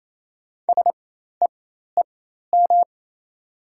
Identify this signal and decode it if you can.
Morse code, heard as a keyed tone, about 28 wpm, spaced as for 13 wpm, 700 Hz. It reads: SEEM
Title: 4x Character Spacing - Top 300 Words - 28wpm